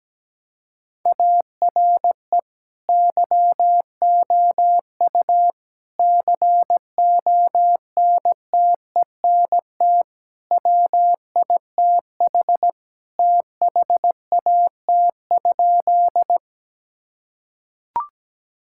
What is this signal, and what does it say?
Morse code, heard as a keyed tone, about 17 wpm, 700 Hz. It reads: ARE YOU CONTENT WITH THAT?
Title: Sentences from Top 1000 Words - Review Part 6 - 17wpm